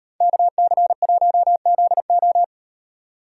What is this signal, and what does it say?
Morse code, heard as a keyed tone, about 38 wpm, 700 Hz. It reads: KC1ZO